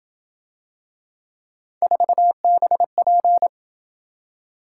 Morse code, heard as a keyed tone, about 27 wpm, 700 Hz. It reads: 4BP